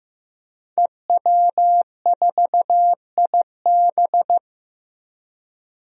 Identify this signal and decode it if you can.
Morse code, heard as a keyed tone, about 16 wpm, 700 Hz. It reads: EW4IB
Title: DXCC-100 Call Signs - 5 Characters - Repeated 3x - 15wpm